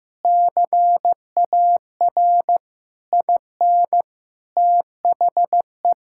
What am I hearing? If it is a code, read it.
Morse code, heard as a keyed tone, about 15 wpm, 700 Hz. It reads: CAR IN THE
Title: Sentences from Top 1600 Words 15wpm